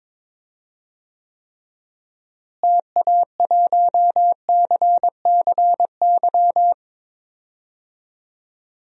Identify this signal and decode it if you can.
Morse code, heard as a keyed tone, about 22 wpm, 700 Hz. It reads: TA1CCY
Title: DXCC-100 Call Signs - 6 Characters - Repeated 2x - 22wpm